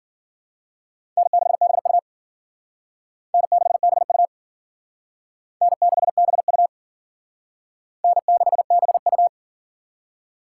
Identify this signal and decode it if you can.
Morse code, heard as a keyed tone, about 52 wpm, 700 Hz. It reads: N6BU N6BU N6BU N6BU